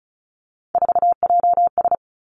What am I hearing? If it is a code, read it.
Morse code, heard as a keyed tone, about 36 wpm, 700 Hz. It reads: 4JS
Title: Three Challenging Letter-Number Combinations 35wpm